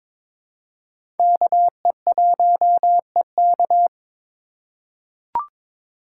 Morse code, heard as a keyed tone, about 22 wpm, 700 Hz. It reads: KE1EK
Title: USA Call Signs - 5 Characters - 2x Character Spacing - 22wpm